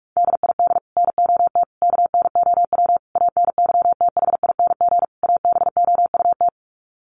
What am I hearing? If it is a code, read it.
Morse code, heard as a keyed tone, about 44 wpm, 700 Hz. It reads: DID NOT KNOW ANYTHING ABOUT
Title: Sets of 5 Words Encore2 45wpm